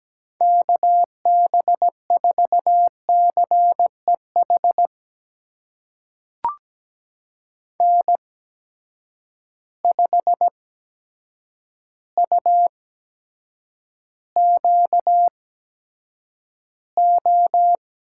Morse code, heard as a keyed tone, about 17 wpm, spaced as for 8 wpm, 700 Hz. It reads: KB4CEH N5UQO
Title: USA Call Signs - 8x Character Spacing - 17wpm